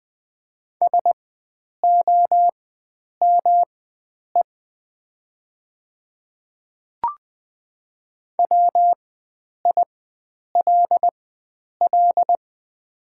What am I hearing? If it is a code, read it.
Morse code, heard as a keyed tone, about 20 wpm, spaced as for 9 wpm, 700 Hz. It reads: SOME WILL
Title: 4x Character Spacing - Top 100 Words - 20wpm